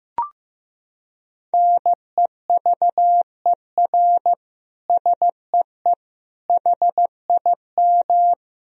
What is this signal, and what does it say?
Morse code, heard as a keyed tone, about 15 wpm, 700 Hz. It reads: NEVER SEE HIM
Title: Sentences from Top 300 Words and Abbreviations 15wpm